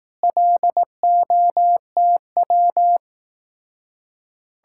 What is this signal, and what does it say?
Morse code, heard as a keyed tone, about 18 wpm, 700 Hz. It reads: LOTW
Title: Speed-Racing Common QSO Phrases Encore - 15wpm